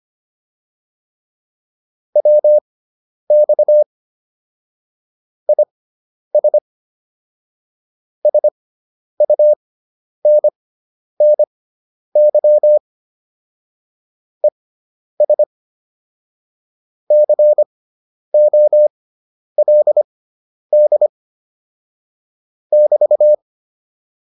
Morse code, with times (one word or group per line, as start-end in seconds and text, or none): 2.2-3.8 WX
5.5-6.6 IS
8.3-12.8 SUNNY
14.4-15.4 ES
17.1-21.1 COLD
22.7-23.3 <BT>